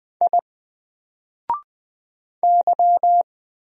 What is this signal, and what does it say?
Morse code, heard as a keyed tone, about 20 wpm, 700 Hz. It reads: I Y
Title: Single Character (Letter:Number:Punctuation:Prosign) Alt 20wpm